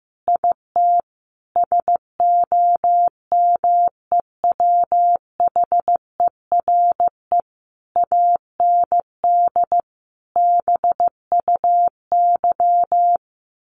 Morse code, heard as a keyed tone, about 15 wpm, 700 Hz. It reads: IT SOMEWHERE AND BUY
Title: Sentences from Top 1400 Words 15wpm